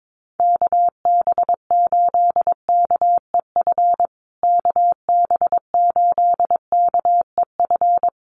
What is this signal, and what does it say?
Morse code, heard as a keyed tone, about 22 wpm, 700 Hz. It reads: KB8KEF KB8KEF
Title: USA Call Signs Repeated 3x 22wpm